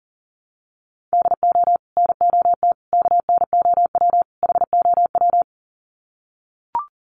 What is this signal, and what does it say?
Morse code, heard as a keyed tone, about 40 wpm, 700 Hz. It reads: DO NOT KNOW HOW
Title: Sets of 4 Words 40wpm